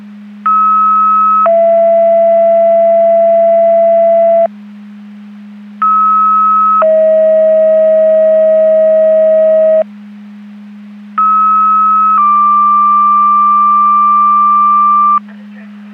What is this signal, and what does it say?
Two-tone page, heard as a one-second tone, then a three-second tone, about 1290 then 670 Hz, about 1290 then 640 Hz, about 1290 then 1180 Hz. It is heard from 0.5 to 15.2 s.
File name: Portage County 154.3850 Fire EMS Paging